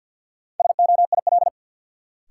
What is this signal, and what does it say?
Morse code, heard as a keyed tone, about 53 wpm, 700 Hz. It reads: SOIL
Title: Sentences from Top 1000 Words - Review Part 4 - New Word Drill - 50wpm